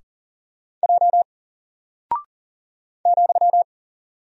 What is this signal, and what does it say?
Morse code, heard as a keyed tone, about 40 wpm, 700 Hz. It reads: J ,
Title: Single Character (Letter:Number:Punctuation:Prosign) Alt 40wpm